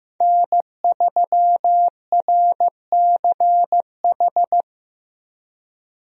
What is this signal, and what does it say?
Morse code, heard as a keyed tone, about 16 wpm, 700 Hz. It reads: N3RCH
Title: USA Call Signs - 5 Characters - Repeated 3x - 15wpm